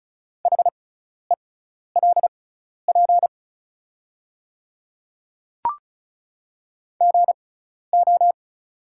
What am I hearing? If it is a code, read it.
Morse code, heard as a keyed tone, about 34 wpm, spaced as for 12 wpm, 700 Hz. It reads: HELP GO
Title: 6x Character Spacing - Top 300 Words - 35wpm